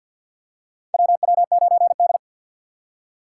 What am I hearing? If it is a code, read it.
Morse code, heard as a keyed tone, about 53 wpm, 700 Hz. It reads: WW9D
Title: USA Call Signs - 4x Character Spacing - 50wpm